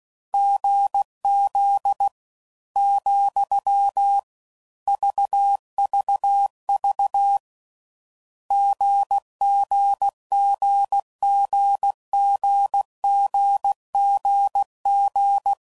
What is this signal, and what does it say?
Morse code, heard as a keyed tone, about 16 wpm, 800 Hz. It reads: GZ , VVV GGGGGGGG
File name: Combinations / Difficult Combos / GZ GZ7